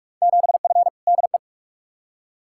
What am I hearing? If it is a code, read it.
Morse code, heard as a keyed tone, about 43 wpm, 700 Hz. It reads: 7F DE
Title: Field Day Exchanges Encore - 45wpm